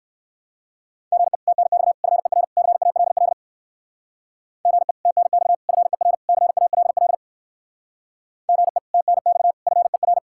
Morse code, heard as a keyed tone, about 59 wpm, 700 Hz. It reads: GE TNX FER CALL GE TNX FER CALL GE TNX FER